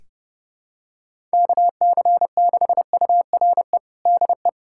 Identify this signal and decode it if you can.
Morse code, heard as a keyed tone, about 32 wpm, 700 Hz. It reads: KC6URE DE
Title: USA Callsign DE Callsign - 6 Characters - 30wpm